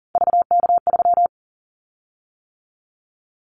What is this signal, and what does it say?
Morse code, heard as a keyed tone, about 42 wpm, 700 Hz. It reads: VK3
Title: Callsign Prefixes for Top 100 DXCC - 40wpm